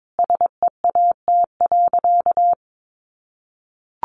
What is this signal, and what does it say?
Morse code, heard as a keyed tone, about 22 wpm, 700 Hz. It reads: SEAT.
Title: Sentences from Top 1500 Words 22wpm